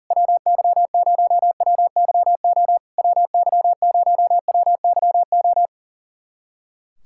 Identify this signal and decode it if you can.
Morse code, heard as a keyed tone, about 37 wpm, 690 Hz. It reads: WY0WYO WY0WYO
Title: SOTA QSO Practice 40wpm